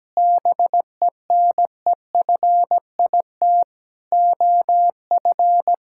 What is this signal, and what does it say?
Morse code, heard as a keyed tone, about 17 wpm, 700 Hz. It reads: BENEFIT OF
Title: Sentences from Top 1000 Words - Review Part 6 - 17wpm